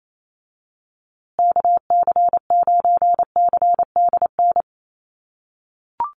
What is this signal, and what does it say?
Morse code, heard as a keyed tone, about 28 wpm, 700 Hz. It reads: KC9CDN
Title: USA Call Signs Repeated 2x 28wpm